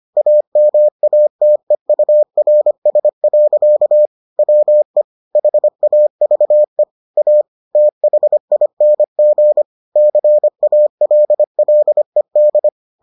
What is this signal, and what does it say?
Morse code, heard as a keyed tone, about 25 wpm, 600 Hz. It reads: AMATEURS. WE HAVE A THING CALLED